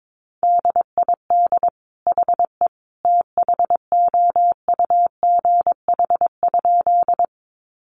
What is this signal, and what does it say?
Morse code, heard as a keyed tone, about 22 wpm, 700 Hz. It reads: DID HE THOUGH?
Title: Sentences from Top 400 Words 22wpm